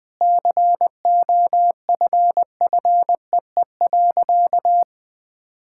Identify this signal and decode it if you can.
Morse code, heard as a keyed tone, about 21 wpm, 700 Hz. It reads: COFFEE.
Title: Sentences with 7-Word Phrases - 20wpm